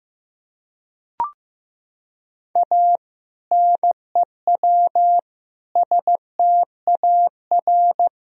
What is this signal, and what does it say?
Morse code, heard as a keyed tone, about 16 wpm, 700 Hz. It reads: A NEW STAR
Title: Sentences from Top 1000 Words - Review Part 2 - 15wpm